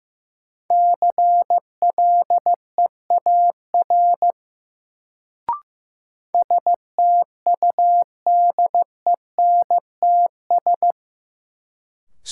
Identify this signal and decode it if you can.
Morse code, heard as a keyed tone, about 15 wpm, 700 Hz. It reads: CLEAR STUDENTS
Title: Sets of 5 Words Encore - New Word Drill - 15wpm